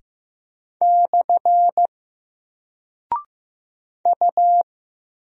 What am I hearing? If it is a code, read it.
Morse code, heard as a keyed tone, about 15 wpm, 700 Hz. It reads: / U
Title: Single Character (Letter:Number:Punctuation:Prosign) Alt 15wpm